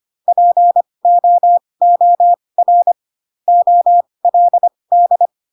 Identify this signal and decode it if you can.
Morse code, heard as a keyed tone, about 25 wpm, spaced as for 20 wpm, 700 Hz. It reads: POOR OLD